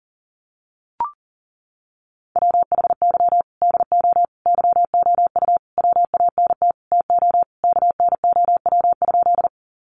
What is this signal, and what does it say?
Morse code, heard as a keyed tone, about 37 wpm, 700 Hz. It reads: WHY DO YOU WANT TO KNOW?
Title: Sentences from Top 200 Words 40wpm